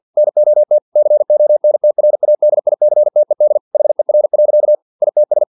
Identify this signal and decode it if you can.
Morse code, heard as a keyed tone, about 50 wpm, 600 Hz. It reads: NOT CONTRADICTED HER. ITS
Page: Never 49wpm